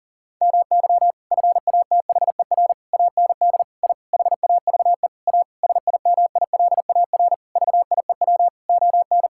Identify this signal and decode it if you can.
Morse code, heard as a keyed tone, about 40 wpm, 700 Hz. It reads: MY FATHER AND I HAVE A SIMILAR VIEW ON
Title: Sentences from Top 1000 Words - Review Part 6 - 40wpm